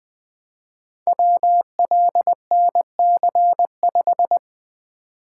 Morse code, heard as a keyed tone, about 20 wpm, 700 Hz. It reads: WLNC5
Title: Code Groups - 5 Letter:Numbers - 20wpm